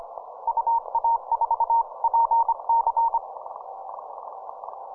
Morse code, heard as a keyed tone, about 25 wpm, 920 Hz. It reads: UA4PC
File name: simple-file-list / ZL8X 15M CW